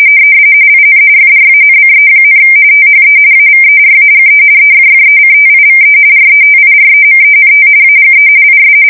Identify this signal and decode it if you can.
RTTY, about 45.45 baud. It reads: IN FRANCE, THE AVERAGE EFFECTIVE RETIREMENT AGE IS
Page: 170 Shift 2125/2295